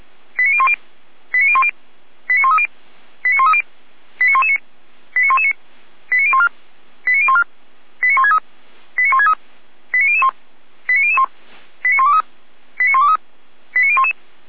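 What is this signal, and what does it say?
Five-tone selective call (ZVEI), heard as quick runs of five tones, tones of about 70 ms. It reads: 89010 89010 89130 89130 89119 89119 89014 89014 89163 89163 89001 89001 89123 89123 89011
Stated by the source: ZVEI five-tone selective call received by SDR.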